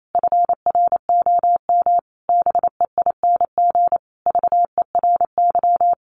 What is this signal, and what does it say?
Morse code, heard as a keyed tone, about 28 wpm, 700 Hz. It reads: FROM BEING VERY
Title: Sentences from Top 1500 Words 28wpm